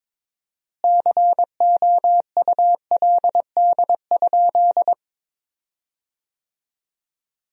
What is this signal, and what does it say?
Morse code, heard as a keyed tone, about 21 wpm, 700 Hz. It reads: COULD?
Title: 4x Word-Spacing - Sentences from Top 400 Words - 22wpm